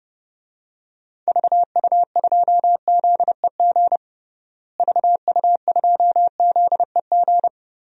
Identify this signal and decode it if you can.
Morse code, heard as a keyed tone, about 30 wpm, 700 Hz. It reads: VU2ZEG VU2ZEG
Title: DXCC-100 Call Signs - 6 Characters - Repeated 3x - 30wpm